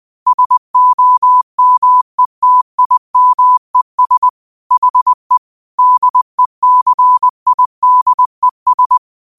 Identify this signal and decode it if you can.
Morse code, heard as a keyed tone, about 20 wpm, 1000 Hz. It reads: SOMETIMES HE DECIDES
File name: qotdfiles / 20wpm